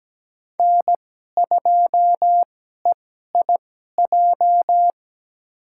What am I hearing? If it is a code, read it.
Morse code, heard as a keyed tone, about 17 wpm, spaced as for 12 wpm, 700 Hz. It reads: N2EIJ
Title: USA Call Signs - 5 Characters - 2x Character Spacing - 17wpm